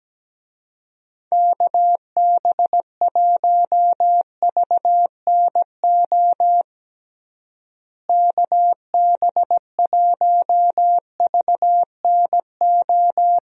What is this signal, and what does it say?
Morse code, heard as a keyed tone, about 17 wpm, 700 Hz. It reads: KB1VNO KB1VNO
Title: USA Call Signs - 6 Characters - Repeated 3x - 17wpm